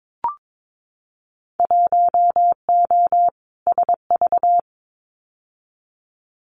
Morse code, heard as a keyed tone, about 22 wpm, 700 Hz. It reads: 1O SV
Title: Winter Field Day Exchanges 22wpm